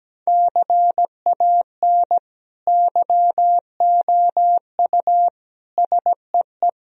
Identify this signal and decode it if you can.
Morse code, heard as a keyed tone, about 17 wpm, 700 Hz. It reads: CAN YOU SEE
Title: Sentences from Top 1000 Words - Review Part 1 - 17wpm